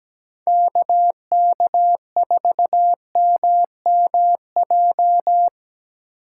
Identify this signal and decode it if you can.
Morse code, heard as a keyed tone, about 17 wpm, 700 Hz. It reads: KK4MMJ
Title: USA Call Signs - 6 Characters - 8x Character Spacing - 17wpm